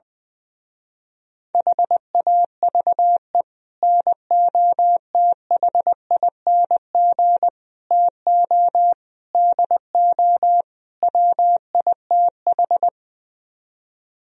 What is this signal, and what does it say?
Morse code, heard as a keyed tone, about 20 wpm, 700 Hz. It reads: HAVE NOTHING TO DO WITH
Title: Speed-Racing - Sets of 5 Words 20wpm